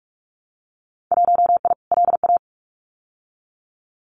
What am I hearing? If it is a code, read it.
Morse code, heard as a keyed tone, about 46 wpm, 700 Hz. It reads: 1I LA
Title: Winter Field Day Exchanges Encore 45wpm